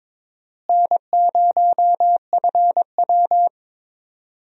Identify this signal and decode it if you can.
Morse code, heard as a keyed tone, about 21 wpm, 700 Hz. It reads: N0FW
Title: USA Call Signs - 4 Characters - Repeated 2x - 22wpm